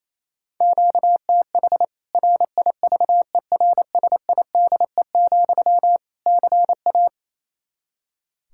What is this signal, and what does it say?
Morse code, heard as a keyed tone, about 27 wpm, 700 Hz. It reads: QTH RIVERSIDE, CA
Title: K1USN SST QSO Practice 28wpm